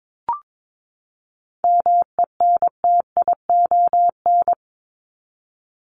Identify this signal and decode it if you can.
Morse code, heard as a keyed tone, about 22 wpm, 700 Hz. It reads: MENTION
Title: Top 1501-1750 Words 22wpm